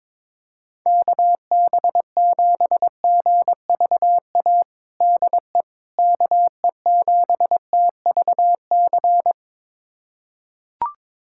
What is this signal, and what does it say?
Morse code, heard as a keyed tone, about 22 wpm, 700 Hz. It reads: KB7GVA DE KE7TVC
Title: USA Callsign DE Callsign - 6 Characters - 22wpm